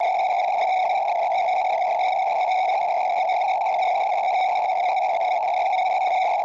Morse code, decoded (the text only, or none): none